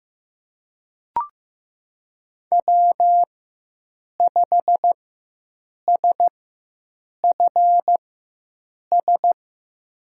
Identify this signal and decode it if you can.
Morse code, heard as a keyed tone, about 15 wpm, spaced as for 7 wpm, 700 Hz. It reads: W5SFS
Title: USA Call Signs - 5 Characters - 4x Character Spacing - 15wpm